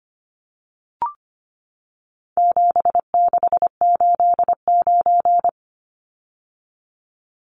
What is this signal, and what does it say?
Morse code, heard as a keyed tone, about 25 wpm, 700 Hz. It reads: 7689
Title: Numbers - 4-Digit 25wpm